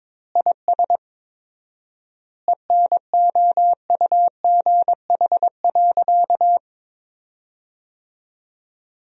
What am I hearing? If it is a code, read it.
Morse code, heard as a keyed tone, about 22 wpm, 700 Hz. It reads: IS ENOUGH.
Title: 4x Word-Spacing - Sentences from Top 400 Words - 22wpm